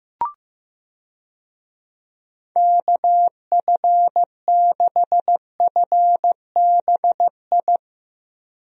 Morse code, heard as a keyed tone, about 15 wpm, 700 Hz. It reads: KF6FBI
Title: USA Call Signs - 6 Characters - 15wpm